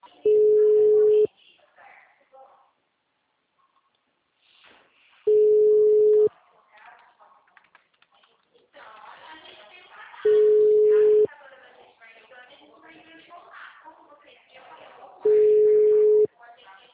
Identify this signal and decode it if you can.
Ringback tone (Europe, 425 Hz), with speech.